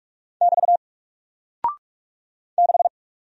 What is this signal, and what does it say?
Morse code, heard as a keyed tone, about 46 wpm, 700 Hz. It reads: <BT> 6
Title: Single Character (Letter:Number:Punctuation:Prosign) Alt 45wpm